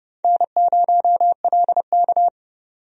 Morse code, heard as a keyed tone, about 32 wpm, 700 Hz. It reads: N0LK
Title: USA Call Signs - 4 Characters - 30wpm